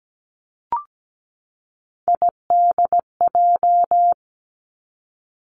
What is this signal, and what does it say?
Morse code, heard as a keyed tone, about 17 wpm, 700 Hz. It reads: IDJ
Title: Three Challenging Letter-Number Combinations 17wpm